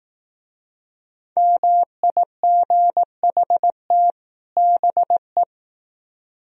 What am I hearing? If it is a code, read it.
Morse code, heard as a keyed tone, about 19 wpm, 700 Hz. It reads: MIGHT BE